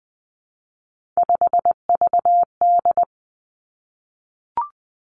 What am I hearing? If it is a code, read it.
Morse code, heard as a keyed tone, about 20 wpm, 700 Hz. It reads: 5VD
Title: Three Challenging Letter-Number Combinations 20wpm